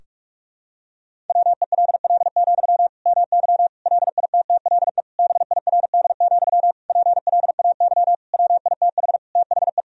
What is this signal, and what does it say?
Morse code, heard as a keyed tone, about 46 wpm, 700 Hz. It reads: WELL, MY LITTLE BIRD, PLAY WITH THE
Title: Sentences from Top 1000 Words - Review Part 2 - 45wpm